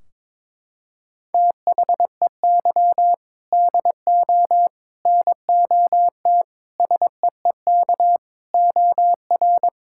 Morse code, heard as a keyed tone, about 22 wpm, 700 Hz. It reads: THEY DO NOT SEEK OR
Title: Sentences from Top 1600 Words 22wpm